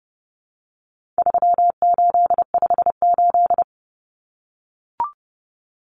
Morse code, heard as a keyed tone, about 30 wpm, 700 Hz. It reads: 3858
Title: Common Cut Numbers - 4-Digit 30wpm